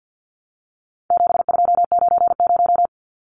Morse code, heard as a keyed tone, about 48 wpm, 700 Hz. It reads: 7290
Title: Numbers - 4-Digit 50wpm